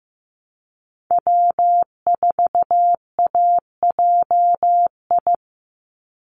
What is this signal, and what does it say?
Morse code, heard as a keyed tone, about 15 wpm, 700 Hz. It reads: W4AJI
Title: USA Call Signs 15wpm